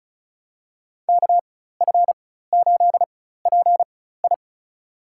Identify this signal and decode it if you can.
Morse code, heard as a keyed tone, about 34 wpm, spaced as for 16 wpm, 700 Hz. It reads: KF8PI
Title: USA Call Signs - 5 Characters - 4x Character Spacing - 35wpm